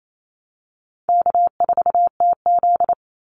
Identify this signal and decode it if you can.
Morse code, heard as a keyed tone, about 28 wpm, 700 Hz. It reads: K4TZ
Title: Easy Call Signs Repeated 3x - 28wpm